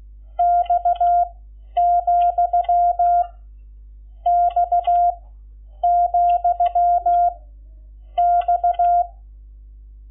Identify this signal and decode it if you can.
Morse code, heard as a keyed tone, about 16 wpm, 700 Hz. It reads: X , X , X
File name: Lecciones / CLASE Nº11